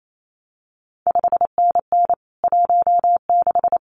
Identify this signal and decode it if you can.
Morse code, heard as a keyed tone, about 28 wpm, 700 Hz. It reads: 5NN 16
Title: CQ WW DX Contest Exchange - 28wpm